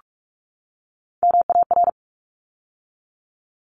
Morse code, heard as a keyed tone, about 44 wpm, 700 Hz. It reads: MAR
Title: ARRL Sections 45wpm